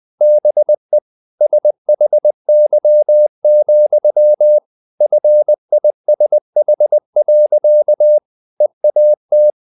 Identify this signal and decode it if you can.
Morse code, heard as a keyed tone, about 20 wpm, 600 Hz. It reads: BE SHY, FISH. EAT
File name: the old man and the sea / the old man and the sea part 1 20wpm